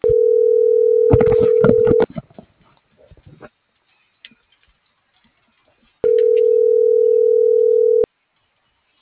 Ringback tone (North America), 440 + 480 Hz.